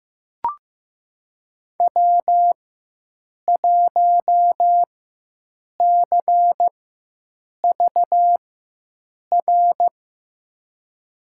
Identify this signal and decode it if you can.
Morse code, heard as a keyed tone, about 15 wpm, spaced as for 7 wpm, 700 Hz. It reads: W1CVR
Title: USA Call Signs - 5 Characters - 4x Character Spacing - 15wpm